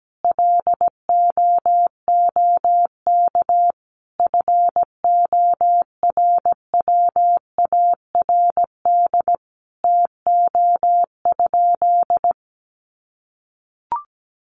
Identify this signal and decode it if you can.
Morse code, heard as a keyed tone, about 17 wpm, 700 Hz. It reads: LOOK FORWARD TO?
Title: Sentences from Top 1300 Words 17wpm